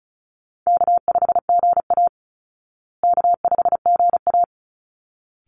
Morse code, heard as a keyed tone, about 34 wpm, 700 Hz. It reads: K5GA K5GA
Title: Easy Call Signs Repeated 2x - 35wpm